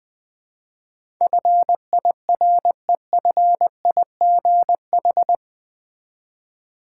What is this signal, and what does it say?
Morse code, heard as a keyed tone, about 20 wpm, 700 Hz. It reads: FIREFIGH
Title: Build A Word - 4 Letters At A Time - 11-Letter Words - 20wpm